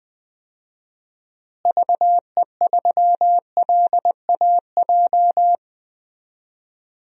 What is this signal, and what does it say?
Morse code, heard as a keyed tone, about 21 wpm, 700 Hz. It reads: VE3LAJ
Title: DXCC-100 Call Signs - 6 Characters - Repeated 2x - 20wpm